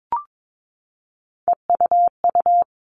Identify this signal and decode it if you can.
Morse code, heard as a keyed tone, about 22 wpm, 700 Hz. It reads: EUU
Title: All Cut Numbers - 3-Digit 22wpm